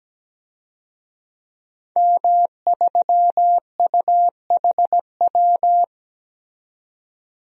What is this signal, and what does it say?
Morse code, heard as a keyed tone, about 17 wpm, 700 Hz. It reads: M3UHW
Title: DXCC-100 Call Signs - 5 Characters - 17wpm